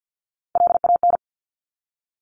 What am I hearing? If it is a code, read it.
Morse code, heard as a keyed tone, about 50 wpm, 700 Hz. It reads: LAN
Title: Top Three Letter Combinations in Words 50wpm